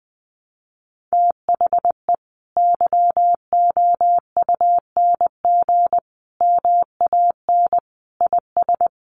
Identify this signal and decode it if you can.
Morse code, heard as a keyed tone, about 20 wpm, 700 Hz. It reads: THE YOUNG MAN IS
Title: Sentences from Top 1200 Words 20wpm